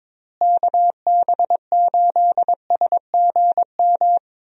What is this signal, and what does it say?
Morse code, heard as a keyed tone, about 22 wpm, 700 Hz. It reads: KB8SGM
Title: USA Call Signs - 6 Characters - Repeated 3x - 22wpm